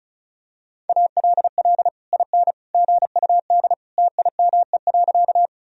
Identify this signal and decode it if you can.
Morse code, heard as a keyed tone, about 34 wpm, 700 Hz. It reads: ALL IN GUD TIME.
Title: Sentences from Top 200 Words and Abbreviations 35wpm